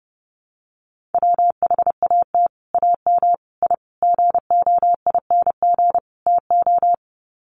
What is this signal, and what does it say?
Morse code, heard as a keyed tone, about 30 wpm, 700 Hz. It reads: WHAT AM I GOING TO
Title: Sets of 5 Words Encore 30wpm